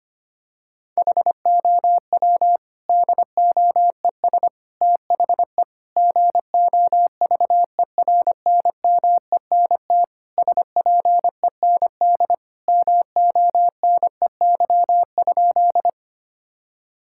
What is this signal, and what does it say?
Morse code, heard as a keyed tone, about 25 wpm, 700 Hz. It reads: HOW DOES THE GOVERNMENT SPEND MONEY?
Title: Sentences from Top 1000 Words - Review Part 5 - 25wpm